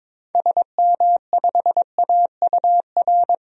Morse code, heard as a keyed tone, about 22 wpm, 700 Hz. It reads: SM5AUR
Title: DXCC-100 Call Signs - 6 Characters - 22wpm